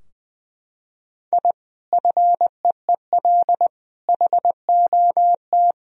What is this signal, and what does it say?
Morse code, heard as a keyed tone, about 20 wpm, 700 Hz. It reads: I FEEL HOT
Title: Sentences from Top 1000 Words - Review Part 3 - 20wpm